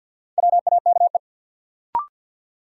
Morse code, heard as a keyed tone, about 48 wpm, 700 Hz. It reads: WAKE
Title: Sets of 4 Words - New Word Drill - 50wpm